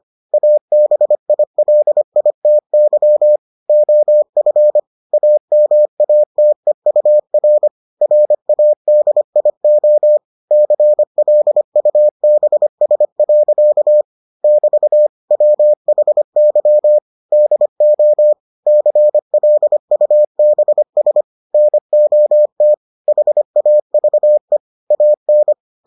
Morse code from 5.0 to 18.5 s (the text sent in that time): AMATEUR RADIO CLUBS. <BT> WHY DO